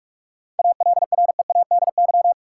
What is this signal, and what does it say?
Morse code, heard as a keyed tone, about 47 wpm, 700 Hz. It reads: ALREADY
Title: Sentences from Top 700 Words - New Word Drill - 45wpm